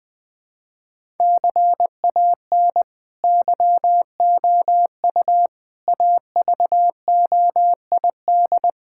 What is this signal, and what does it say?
Morse code, heard as a keyed tone, about 20 wpm, 700 Hz. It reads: CAN YOU AVOID
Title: Sentences from Top 1000 Words - Review Part 6 - 20wpm